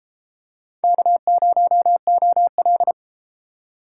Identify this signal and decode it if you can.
Morse code, heard as a keyed tone, about 30 wpm, 700 Hz. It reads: K0OL